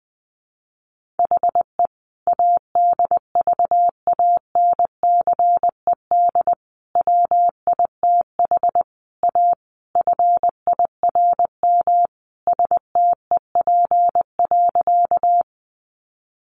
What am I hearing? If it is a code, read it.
Morse code, heard as a keyed tone, about 20 wpm, 700 Hz. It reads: HE ADVANCED WITH A FIRM STEP.
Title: Sentences from Top 1400 Words 20wpm